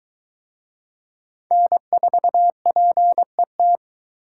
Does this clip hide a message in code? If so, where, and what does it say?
Yes: Morse code from 1.5 to 3.8 s, heard as a keyed tone, about 20 wpm, 700 Hz. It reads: N4PET